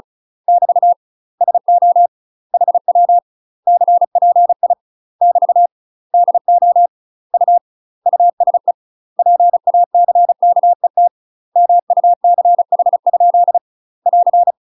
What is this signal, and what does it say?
Morse code, heard as a keyed tone, about 36 wpm, 700 Hz. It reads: <BT> SO HW CPI <BT> DO U USE PACKET MUCH? <AR>
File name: UK MASTER QSOs TRAINING / 35WPM / UK MASTER QSOs 0010